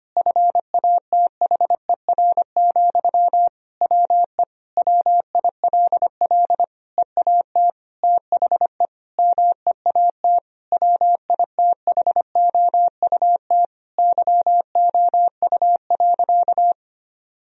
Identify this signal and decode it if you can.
Morse code, heard as a keyed tone, about 25 wpm, 700 Hz. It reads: FATHER, WE WILL EAT THE MEAT WITHOUT YOU.
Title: Sentences from Top 1000 Words - Review Part 3 - 25wpm